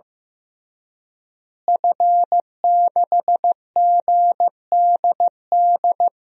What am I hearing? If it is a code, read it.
Morse code, heard as a keyed tone, about 16 wpm, 700 Hz. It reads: F6GDD
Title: DXCC-100 Call Signs - 5 Characters - Repeated 3x - 15wpm